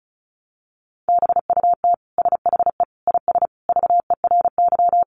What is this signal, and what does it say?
Morse code, heard as a keyed tone, about 34 wpm, 700 Hz. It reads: BUT SHE IS VERY
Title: Sentences from Top 400 Words 35wpm